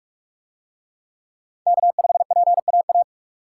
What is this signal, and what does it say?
Morse code, heard as a keyed tone, about 44 wpm, 700 Hz. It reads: K5PAA